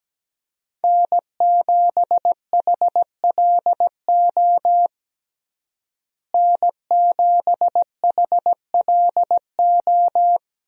Morse code, heard as a keyed tone, about 17 wpm, 700 Hz. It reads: N7HLO N7HLO
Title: USA Call Signs - 5 Characters - Repeated 2x - 17wpm